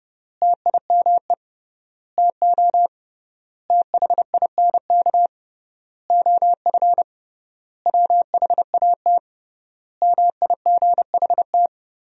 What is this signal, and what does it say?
Morse code, heard as a keyed tone, about 30 wpm, 700 Hz. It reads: TIME TO THINK OF WHAT MIGHT
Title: 3x Word-Spacing - Sentences from Top 400 Words - 30wpm